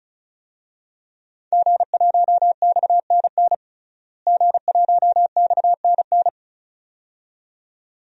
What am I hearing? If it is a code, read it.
Morse code, heard as a keyed tone, about 36 wpm, 700 Hz. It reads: G1XNN G1XNN
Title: DXCC-100 Call Signs - 5 Characters - Repeated 2x - 35wpm